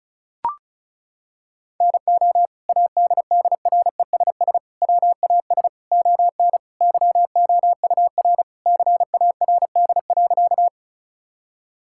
Morse code, heard as a keyed tone, about 34 wpm, 700 Hz. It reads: NO ADDRESS WAS ON YOUR CARD.
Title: Sentences from Top 1000 Words - Review Part 6 - 35wpm